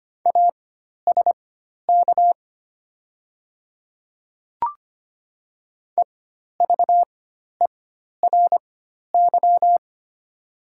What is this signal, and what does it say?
Morse code, heard as a keyed tone, about 25 wpm, spaced as for 12 wpm, 700 Hz. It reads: ASK EVERY
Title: 4x Character Spacing - Top 300 Words - 25wpm